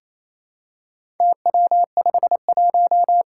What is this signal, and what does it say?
Morse code, heard as a keyed tone, about 28 wpm, 700 Hz. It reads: TW51
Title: Four Challenging Letter-Number Combinations 28wpm